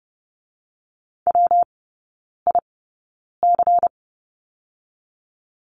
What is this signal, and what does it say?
Morse code, heard as a keyed tone, about 30 wpm, 700 Hz. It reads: W I C
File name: Lesson 19e - Letter w - Three Characters 30wpm